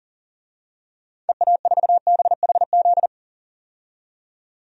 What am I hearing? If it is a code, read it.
Morse code, heard as a keyed tone, about 40 wpm, 700 Hz. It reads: EA4BHZ
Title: DXCC-100 Call Signs - 6 Characters - 40wpm